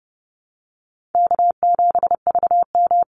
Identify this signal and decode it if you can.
Morse code, heard as a keyed tone, about 30 wpm, 700 Hz. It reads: K7VM
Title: Call Signs Repeated 2x - 30wpm